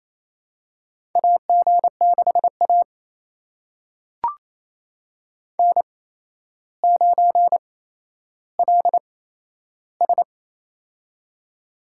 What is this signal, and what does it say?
Morse code, heard as a keyed tone, about 28 wpm, spaced as for 12 wpm, 700 Hz. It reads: AG6A N9LS
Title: USA Call Signs - 8x Character Spacing - 28wpm